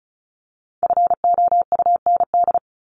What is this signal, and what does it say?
Morse code, heard as a keyed tone, about 34 wpm, 700 Hz. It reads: FOUND
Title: Top 300 Words Repeated 35wpm